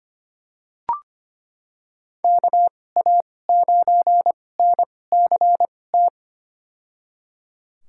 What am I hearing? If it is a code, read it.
Morse code, heard as a keyed tone, about 25 wpm, spaced as for 18 wpm, 700 Hz. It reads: KA9NCT
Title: USA Call Signs - 6 Characters - 2x Character Spacing - 25wpm